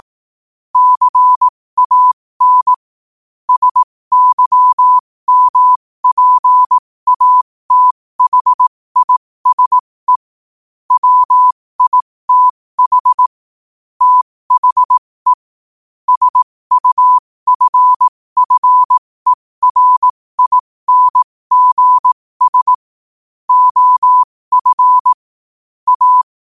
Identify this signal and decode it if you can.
Morse code, heard as a keyed tone, about 18 wpm, spaced as for 15 wpm, 1000 Hz. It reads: CAN SYMPATHISE WITH THE SUFFERINGS OF A
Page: New quotes every day in morse code at 15 Words per minute.